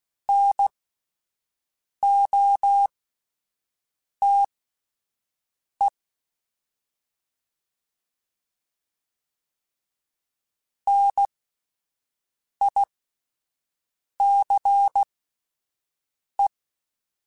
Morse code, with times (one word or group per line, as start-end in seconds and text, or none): 0.3-5.9 NOTE
10.9-16.5 NICE